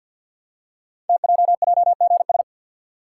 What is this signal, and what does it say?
Morse code, heard as a keyed tone, about 50 wpm, 700 Hz. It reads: TJJGS
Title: Code Groups - 5 Letters - 50wpm